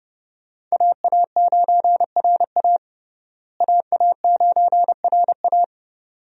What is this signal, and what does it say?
Morse code, heard as a keyed tone, about 32 wpm, 700 Hz. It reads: AA9RA AA9RA
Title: USA Call Signs - 5 Characters - Repeated 2x - 30wpm